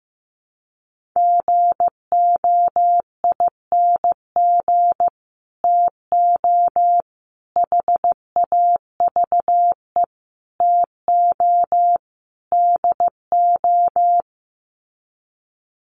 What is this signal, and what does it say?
Morse code, heard as a keyed tone, about 15 wpm, 700 Hz. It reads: GOING TO HAVE TO DO
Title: Sets of 5 Words Encore 15wpm